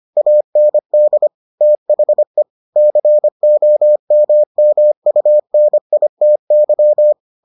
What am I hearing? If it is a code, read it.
Morse code, heard as a keyed tone, about 25 wpm, 600 Hz. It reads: AND THE COMMUNITY